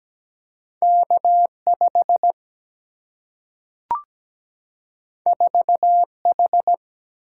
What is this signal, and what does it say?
Morse code, heard as a keyed tone, about 17 wpm, 700 Hz. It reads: K5 4H
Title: Single-Letter Single-Number 17wpm